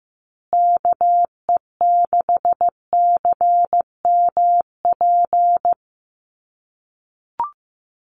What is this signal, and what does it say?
Morse code, heard as a keyed tone, about 15 wpm, 700 Hz. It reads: KE6CMP
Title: USA Call Signs Repeated 3x 15wpm